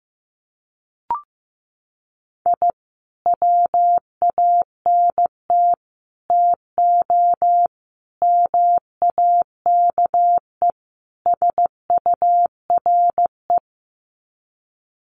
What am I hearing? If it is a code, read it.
Morse code, heard as a keyed tone, about 15 wpm, 700 Hz. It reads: I WANT TO MAKE SURE
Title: Sets of 5 Words Encore2 15wpm